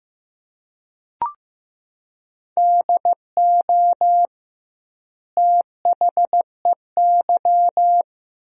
Morse code, heard as a keyed tone, about 15 wpm, 700 Hz. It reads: DO THEY
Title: 2x Word-Spacing - Sentences from Top 100 Words - 15wpm